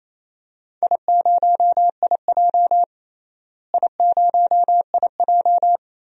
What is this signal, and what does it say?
Morse code, heard as a keyed tone, about 28 wpm, 700 Hz. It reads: I0IJ I0IJ
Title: DXCC-100 Call Signs - 4 Characters - Repeated 2x - 28wpm